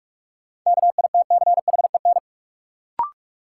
Morse code, heard as a keyed tone, about 46 wpm, 700 Hz. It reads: KITCHEN
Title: Sentences from Top 1400 Words - New Word Drill Alt - 45wpm